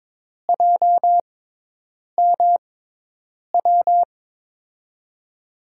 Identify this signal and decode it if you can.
Morse code, heard as a keyed tone, about 22 wpm, spaced as for 8 wpm, 700 Hz. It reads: JMW
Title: USA Call Signs - 6 Characters - 6x Character Spacing - 22wpm